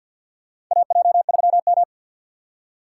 Morse code, heard as a keyed tone, about 46 wpm, 700 Hz. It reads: AJ3M
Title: USA Call Signs - 6x Character Spacing - 50wpm